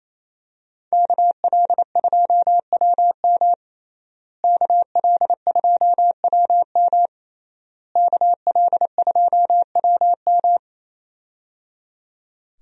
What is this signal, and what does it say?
Morse code, heard as a keyed tone, about 27 wpm, 700 Hz. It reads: KL2WM KL2WM KL2WM
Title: USA Call Signs - 5 Characters - Repeated 3x - 28wpm